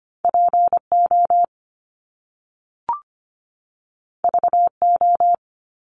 Morse code, heard as a keyed tone, about 25 wpm, 700 Hz. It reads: PO VO
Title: Top Two Letter Combinations in Words 25wpm